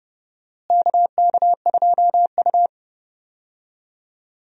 Morse code, heard as a keyed tone, about 30 wpm, 700 Hz. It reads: KK2U
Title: USA Call Signs - 4 Characters - Repeated 2x - 30wpm